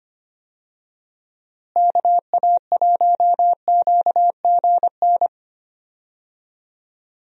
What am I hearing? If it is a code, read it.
Morse code, heard as a keyed tone, about 25 wpm, 700 Hz. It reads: KA1QGN
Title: USA Call Signs - 6 Characters - Repeated 3x - 25wpm